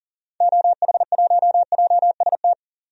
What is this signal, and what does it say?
Morse code, heard as a keyed tone, about 40 wpm, 700 Hz. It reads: OH1JST
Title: DXCC-100 Call Signs - 6 Characters - 40wpm